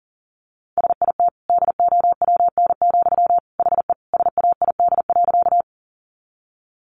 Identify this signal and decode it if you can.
Morse code, heard as a keyed tone, about 40 wpm, 700 Hz. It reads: SIT DOWN, HE SAID.
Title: Sentences from Top 500 Words 40wpm